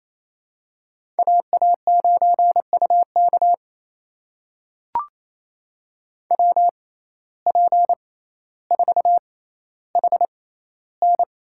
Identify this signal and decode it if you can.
Morse code, heard as a keyed tone, about 28 wpm, spaced as for 14 wpm, 700 Hz. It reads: AA9UK WP4HN
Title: USA Call Signs - 5 Characters - 6x Character Spacing - 28wpm